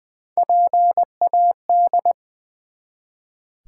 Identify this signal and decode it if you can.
Morse code, heard as a keyed tone, about 20 wpm, 700 Hz. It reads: PAD
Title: Build A Word - 1 Letter At A Time - 3-Letter Words - 20wpm